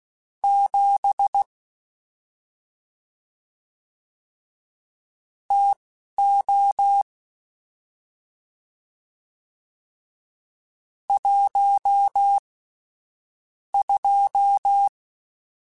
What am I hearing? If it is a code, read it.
Morse code, heard as a keyed tone, about 16 wpm, spaced as for 7 wpm, 800 Hz. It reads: 7 TO 12
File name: Spaced Characters / Spaced Characters-mp3s / Weather 5 22 sp @10